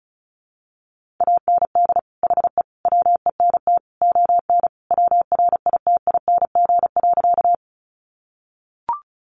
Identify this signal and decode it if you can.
Morse code, heard as a keyed tone, about 34 wpm, 700 Hz. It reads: AND HE WENT ON WRITING.
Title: Sentences from Top 800 Words 35wpm